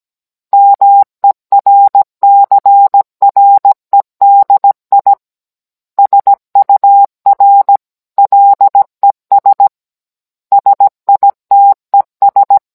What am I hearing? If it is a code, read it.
Morse code, heard as a keyed tone, about 17 wpm, 800 Hz. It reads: MERCREDI SUR LES SITES